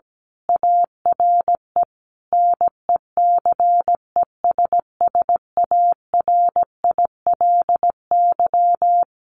Morse code, heard as a keyed tone, about 17 wpm, 700 Hz. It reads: ARE NECESSARILY
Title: Sentences from Top 1400 Words 17wpm